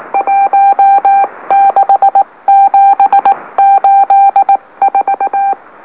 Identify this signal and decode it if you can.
Morse code, heard as a keyed tone, about 19 wpm, spaced as for 17 wpm, 790 Hz. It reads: 16784
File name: M04- -2345utc-LOLO